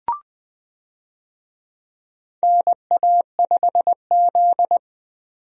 Morse code, heard as a keyed tone, about 20 wpm, 700 Hz. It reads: NA5Z
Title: USA Call Signs - 4 Characters - 20wpm